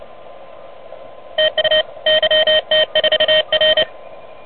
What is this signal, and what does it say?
Morse code, heard as a keyed tone, about 27 wpm, 620 Hz. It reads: TU YT4R
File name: truSDX